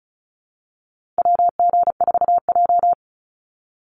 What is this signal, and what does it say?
Morse code, heard as a keyed tone, about 34 wpm, 700 Hz. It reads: WG4J
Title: USA Call Signs 35wpm